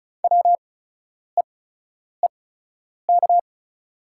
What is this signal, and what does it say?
Morse code, heard as a keyed tone, about 36 wpm, spaced as for 10 wpm, 700 Hz. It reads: WEEK
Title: 8x Character Spacing - Top 500 Words - 35wpm